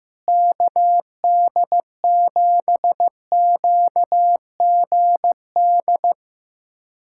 Morse code, heard as a keyed tone, about 15 wpm, 700 Hz. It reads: KD7QGD
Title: USA Callsign DE Callsign - 6 Characters - 15wpm